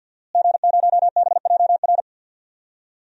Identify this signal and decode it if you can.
Morse code, heard as a keyed tone, about 50 wpm, 700 Hz. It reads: G0BJR
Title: DXCC-100 Call Signs - 5 Characters - Repeated 2x - 50wpm